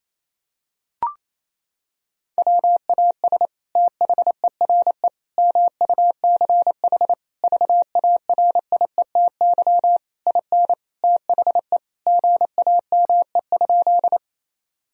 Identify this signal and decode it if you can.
Morse code, heard as a keyed tone, about 28 wpm, 700 Hz. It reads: WAS THERE MUCH VARIETY IN THE GAME?
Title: Sentences from Top 1000 Words - Review Part 4 - 28wpm